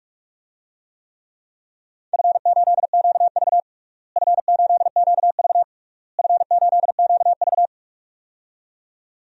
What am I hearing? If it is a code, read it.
Morse code, heard as a keyed tone, about 46 wpm, 700 Hz. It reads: F8QV F8QV F8QV
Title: DXCC-100 Call Signs - 4 Characters - Repeated 3x - 45wpm